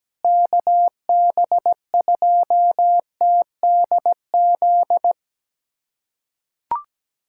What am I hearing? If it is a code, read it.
Morse code, heard as a keyed tone, about 17 wpm, 700 Hz. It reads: KB2TDZ
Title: USA Callsign DE Callsign - 6 Characters - 17wpm